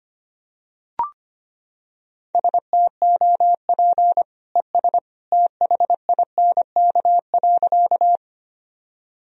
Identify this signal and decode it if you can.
Morse code, heard as a keyed tone, about 25 wpm, 700 Hz. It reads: STOP ES THINK.
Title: Sentences from Top 300 Words and Abbreviations 25wpm